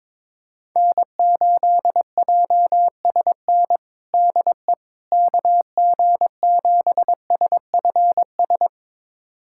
Morse code, heard as a keyed tone, about 21 wpm, 700 Hz. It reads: N8JSN DE KG7SFS
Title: USA Callsign DE Callsign 22wpm